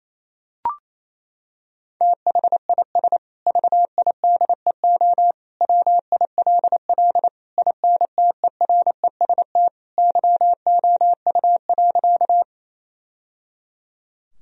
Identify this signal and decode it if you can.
Morse code, heard as a keyed tone, about 27 wpm, 700 Hz. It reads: THIS VIDEO WILL INTEREST YOU.
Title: Sentences from Top 1000 Words - Review Part 4 - 28wpm